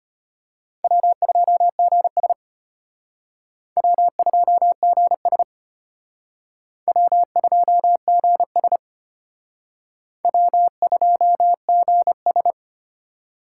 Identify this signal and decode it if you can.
Morse code, heard as a keyed tone, about 32 wpm, 700 Hz. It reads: W2GS W2GS W2GS W2GS